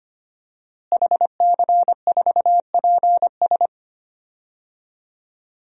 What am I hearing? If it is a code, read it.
Morse code, heard as a keyed tone, about 25 wpm, 700 Hz. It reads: HC4PS
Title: DXCC-100 Call Signs - 5 Characters - Repeated 3x - 25wpm